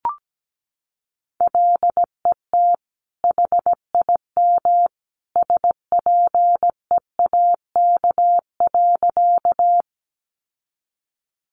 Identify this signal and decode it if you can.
Morse code, heard as a keyed tone, about 17 wpm, 700 Hz. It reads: LET HIM SPEAK.
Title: Sentences from Top 1100 Words 17wpm